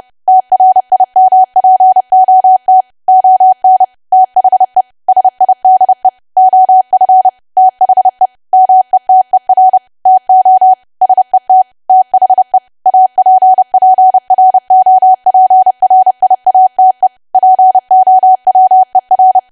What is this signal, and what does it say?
Morse code, heard as a keyed tone, about 30 wpm, 750 Hz. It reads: TRIMPOT ON THE SIDE OF THE METER TO SET THE APPROPRIATE POWER
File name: cwa / W1AW-Level3 / 30WPM